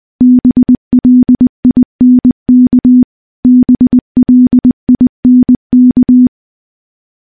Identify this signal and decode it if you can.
Morse code, heard as a keyed tone, about 19 wpm, 250 Hz. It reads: BLINK BLINK
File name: morse / morse-training-20WPM